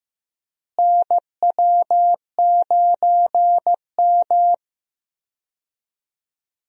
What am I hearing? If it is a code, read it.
Morse code, heard as a keyed tone, about 15 wpm, 700 Hz. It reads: NW9M
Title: USA Call Signs - 4 Characters - Repeated 3x - 15wpm